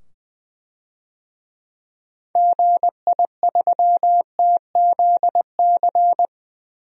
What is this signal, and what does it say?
Morse code, heard as a keyed tone, about 20 wpm, 700 Hz. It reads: GI3TZC
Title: DXCC-100 Call Signs - 6 Characters - 20wpm